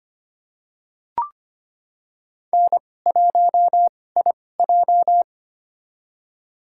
Morse code, heard as a keyed tone, about 25 wpm, spaced as for 18 wpm, 700 Hz. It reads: N1IJ
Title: USA Call Signs - 4 Characters - 2x Character Spacing - 25wpm